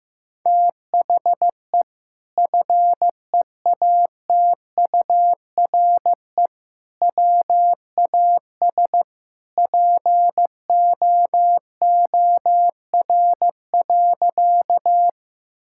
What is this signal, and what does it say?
Morse code, heard as a keyed tone, about 16 wpm, 700 Hz. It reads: THE FEATURE WAS POOR.
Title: Sentences from Top 1000 Words - Review Part 5 - 15wpm